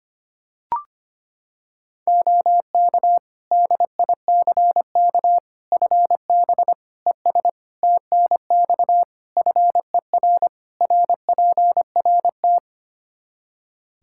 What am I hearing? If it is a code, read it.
Morse code, heard as a keyed tone, about 25 wpm, 700 Hz. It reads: OK DICK FB ES TNX FER RPRT
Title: Practice QSO - Exchange 2 - Info - 25wpm